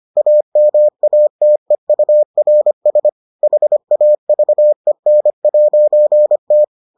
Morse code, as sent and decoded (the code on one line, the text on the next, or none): .- -- .- - . ..- .-. ... / .... .- ...- . -. .----. -
AMATEURS HAVEN'T